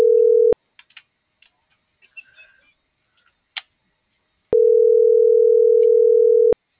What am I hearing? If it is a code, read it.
Ringback tone (North America), 440 + 480 Hz.